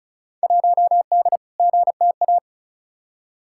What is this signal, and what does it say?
Morse code, heard as a keyed tone, about 36 wpm, 700 Hz. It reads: 1D GTA
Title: Field Day Exchanges - 35wpm